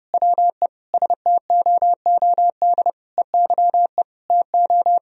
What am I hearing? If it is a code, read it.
Morse code, heard as a keyed tone, about 30 wpm, 700 Hz. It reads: WE STOOD EYE TO
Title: Sentences from Top 1000 Words - Review Part 2 - 30wpm